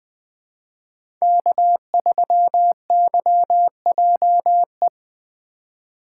Morse code, heard as a keyed tone, about 20 wpm, 700 Hz. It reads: K3YJE
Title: USA Call Signs - 5 Characters - 2x Character Spacing - 20wpm